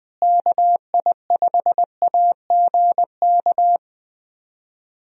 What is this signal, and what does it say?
Morse code, heard as a keyed tone, about 20 wpm, 700 Hz. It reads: KI5AGK
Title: USA Call Signs - 6 Characters - Repeated 2x - 20wpm